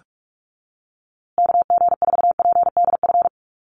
Morse code, heard as a keyed tone, about 44 wpm, 700 Hz. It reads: KG4PDF
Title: USA Call Signs Repeated 3x 45wpm